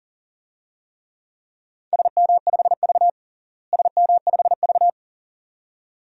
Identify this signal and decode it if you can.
Morse code, heard as a keyed tone, about 40 wpm, 700 Hz. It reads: SM5V SM5V
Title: DXCC-100 Call Signs - 4 Characters - Repeated 2x - 40wpm